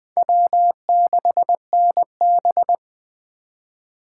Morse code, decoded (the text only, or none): W6NB